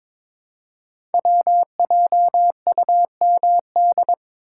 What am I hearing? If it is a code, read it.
Morse code, heard as a keyed tone, about 23 wpm, 700 Hz. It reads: WJUMD
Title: Code Groups - 5 Letters - 22wpm